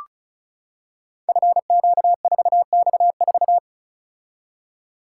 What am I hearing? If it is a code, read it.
Morse code, heard as a keyed tone, about 36 wpm, 700 Hz. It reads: FQ4X4
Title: Code Groups - 5 Letter:Numbers - 35wpm